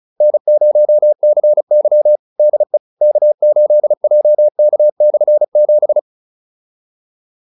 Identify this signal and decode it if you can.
Morse code, heard as a keyed tone, about 35 wpm, 600 Hz. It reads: N0CY DE K8JK/7
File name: QSO310 35